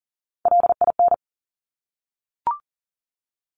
Morse code, heard as a keyed tone, about 41 wpm, 700 Hz. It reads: LIN
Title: Top Three Letter Combinations in Words 40wpm